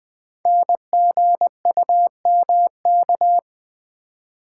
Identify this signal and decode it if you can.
Morse code, heard as a keyed tone, about 20 wpm, 700 Hz. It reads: NGUMK
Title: Code Groups - 5 Letters - 20wpm